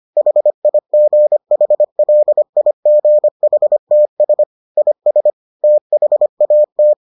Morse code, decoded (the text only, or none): HIGHLIGHTS IS THAT